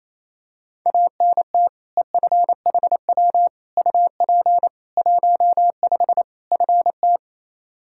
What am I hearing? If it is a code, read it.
Morse code, heard as a keyed tone, about 28 wpm, 700 Hz. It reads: ANT EFHW UP 15 FT
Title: Practice QSO - Exchange 2 - Info - 28wpm